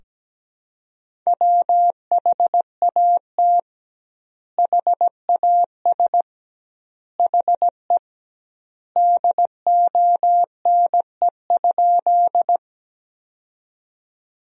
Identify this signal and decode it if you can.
Morse code, heard as a keyed tone, about 17 wpm, 700 Hz. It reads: WHAT HAS HE DONE?
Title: 2x Word-Spacing - Sentences from Top 500 Words - 17wpm